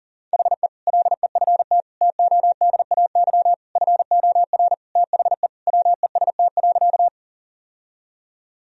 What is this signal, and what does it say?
Morse code, heard as a keyed tone, about 40 wpm, 700 Hz. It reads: HE LEFT TODAY FOR THE WEST.
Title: Sentences from Top 1000 Words - Review Part 2 - 40wpm